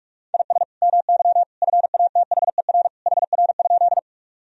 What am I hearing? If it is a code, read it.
Morse code, heard as a keyed tone, about 46 wpm, 700 Hz. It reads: IS MY FATHER HR?
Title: Sentences from Top 300 Words and Abbreviations 45wpm